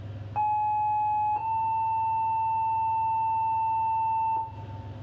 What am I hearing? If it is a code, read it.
Two-tone page, heard as a one-second tone, then a three-second tone, about 850 then 890 Hz. It is heard from 0.4 to 4.5 s.